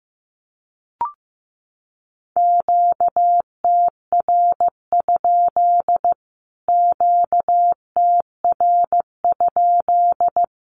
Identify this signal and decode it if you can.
Morse code, heard as a keyed tone, about 15 wpm, 700 Hz. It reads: QTR? QTR?
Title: Q-Codes Repeated 3x 15wpm